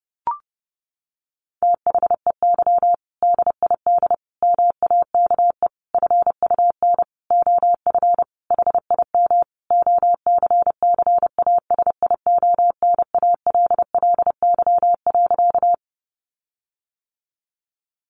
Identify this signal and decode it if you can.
Morse code, heard as a keyed tone, about 30 wpm, 700 Hz. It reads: THEY DID MAKE FUN OF HIM OCCASIONALLY.
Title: Sentences from Top 1300 Words 30wpm